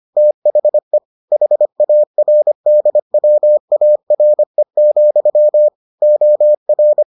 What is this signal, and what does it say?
Morse code, heard as a keyed tone, about 25 wpm, 600 Hz. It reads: THE HARDWARE, OR